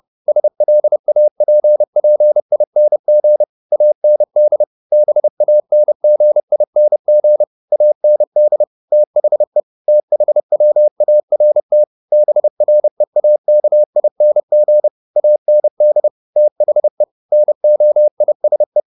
Morse code, heard as a keyed tone, about 30 wpm, 600 Hz. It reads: SLAPPING AND BANGING AND THE THWART BREAKING AND THE NOISE
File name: the old man and the sea / the old man and the sea 1 30wpm